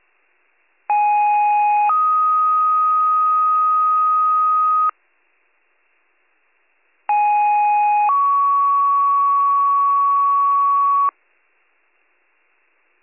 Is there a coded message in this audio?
Yes: two-tone page from 0.9 to 11.1 s, heard as a one-second tone, then a three-second tone, about 830 then 1250 Hz, about 830 then 1120 Hz.